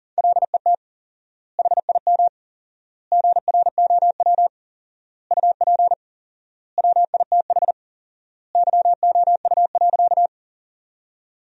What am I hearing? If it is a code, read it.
Morse code, heard as a keyed tone, about 37 wpm, 700 Hz. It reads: LET HIM GROW UP WITH YOU.
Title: 4x Word-Spacing - Sentences from Top 300 Words - 40wpm